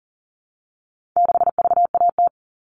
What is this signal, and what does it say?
Morse code, heard as a keyed tone, about 40 wpm, 700 Hz. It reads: 6VAT
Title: All Cut Numbers - 4-Digit 40wpm